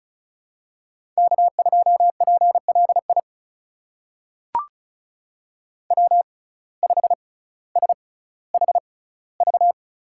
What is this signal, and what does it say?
Morse code, heard as a keyed tone, about 36 wpm, spaced as for 18 wpm, 700 Hz. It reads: K2PLI W5SHV
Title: USA Call Signs - 5 Characters - 6x Character Spacing - 35wpm